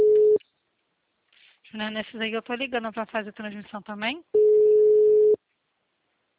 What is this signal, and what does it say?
Ringback tone (Europe, 425 Hz), with speech.
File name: out- -2010- - - .41190